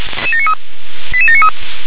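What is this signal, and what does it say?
Five-tone selective call (ZVEI), heard as quick runs of five tones, tones of about 70 ms. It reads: 88793
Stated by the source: ZVEI five-tone call received via SDR.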